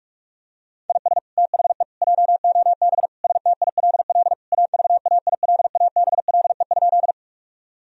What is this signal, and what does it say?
Morse code, heard as a keyed tone, about 41 wpm, 700 Hz. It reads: IS THE JOB STILL AVAILABLE?
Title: Sentences from Top 1000 Words - Review Part 3 - 45wpm